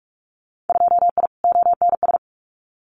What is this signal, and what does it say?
Morse code, heard as a keyed tone, about 44 wpm, 700 Hz. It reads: 2I ONS
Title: Winter Field Day Exchanges Encore 45wpm